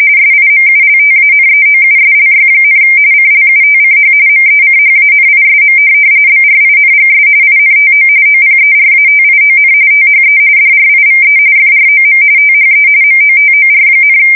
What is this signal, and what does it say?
RTTY, about 45.45 baud. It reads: SIMULTANEOUSLY, CLUTCH MECHANISM BRINGS INTO OPERATION THE DRIVE FROM THE MOTOR,
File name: fsk170 / morkrum